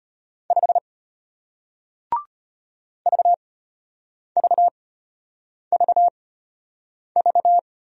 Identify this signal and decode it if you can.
Morse code, heard as a keyed tone, about 33 wpm, spaced as for 8 wpm, 700 Hz. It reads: 5 VVVV